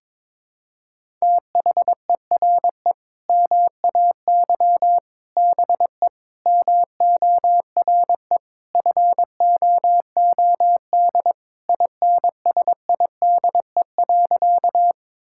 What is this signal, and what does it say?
Morse code, heard as a keyed tone, about 22 wpm, 700 Hz. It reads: THERE MAY BE MORE FOOD INSIDE.
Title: Sentences from Top 1000 Words - Review Part 6 - 22wpm